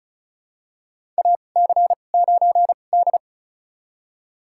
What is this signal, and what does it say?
Morse code, heard as a keyed tone, about 36 wpm, spaced as for 25 wpm, 700 Hz. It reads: AC9D
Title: USA Call Signs - 2x Character Spacing - 35wpm